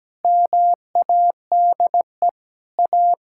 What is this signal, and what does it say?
Morse code, heard as a keyed tone, about 17 wpm, 700 Hz. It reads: MADE A
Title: Sentences from Top 1000 Words - Review Part 5 - 17wpm